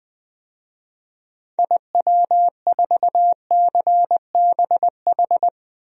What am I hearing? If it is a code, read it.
Morse code, heard as a keyed tone, about 21 wpm, 700 Hz. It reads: IW4CBH
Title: DXCC-100 Call Signs - 6 Characters - 20wpm